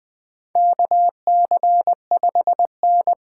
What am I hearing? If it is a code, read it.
Morse code, heard as a keyed tone, about 20 wpm, 700 Hz. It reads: KC5N
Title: USA Call Signs - 6 Characters - 8x Character Spacing - 20wpm